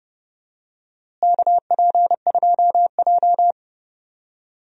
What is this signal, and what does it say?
Morse code, heard as a keyed tone, about 32 wpm, 700 Hz. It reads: KP2J
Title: DXCC-100 Call Signs - 4 Characters - Repeated 2x - 30wpm